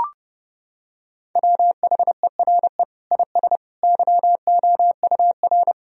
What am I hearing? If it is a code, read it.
Morse code, heard as a keyed tone, about 32 wpm, 700 Hz. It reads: WHERE IS YOUR
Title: Sentences from Top 1000 Words - Review Part 1 - 30wpm